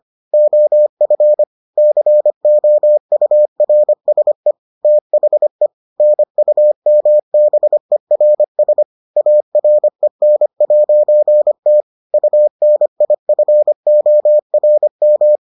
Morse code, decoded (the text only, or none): OF COURSE THE NUMBERS AREN'T UNIFORM